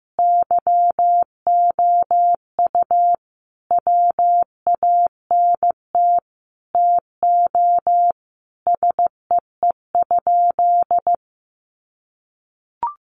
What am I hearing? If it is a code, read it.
Morse code, heard as a keyed tone, about 15 wpm, 700 Hz. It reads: YOU WANT TO SEE?
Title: Sentences from Top 1100 Words 15wpm